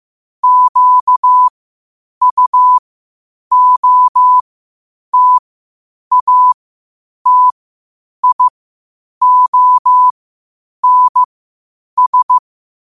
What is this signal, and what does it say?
Morse code, heard as a keyed tone, about 15 wpm, spaced as for 8 wpm, 1000 Hz. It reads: QUOTATIONS
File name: qotdfiles / 8wpm